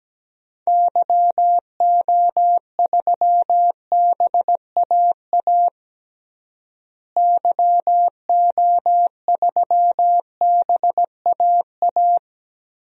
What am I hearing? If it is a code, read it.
Morse code, heard as a keyed tone, about 17 wpm, 700 Hz. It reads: YO3BAA YO3BAA
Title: DXCC-100 Call Signs - 6 Characters - Repeated 3x - 17wpm